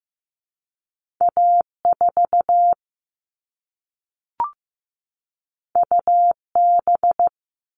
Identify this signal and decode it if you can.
Morse code, heard as a keyed tone, about 15 wpm, 700 Hz. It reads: A4 UB
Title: All Cut Numbers - 2-Digit 15wpm